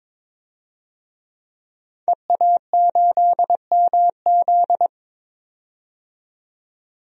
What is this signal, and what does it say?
Morse code, heard as a keyed tone, about 22 wpm, 700 Hz. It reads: EA8MZ
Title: DXCC-100 Call Signs - 5 Characters - 22wpm